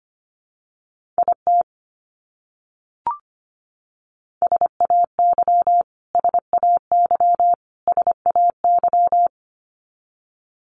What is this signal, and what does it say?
Morse code, heard as a keyed tone, about 25 wpm, 700 Hz. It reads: IT SAY SAY SAY
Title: Top 300 Words Repeated 25wpm